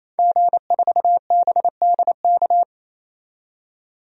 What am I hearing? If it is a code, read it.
Morse code, heard as a keyed tone, about 28 wpm, 700 Hz. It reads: G4BDK
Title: DXCC-100 Call Signs - 5 Characters - 28wpm